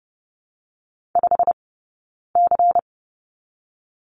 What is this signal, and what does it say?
Morse code, heard as a keyed tone, about 30 wpm, 700 Hz. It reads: 5 C
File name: Lesson 16d - Letter c - Two Characters 30wpm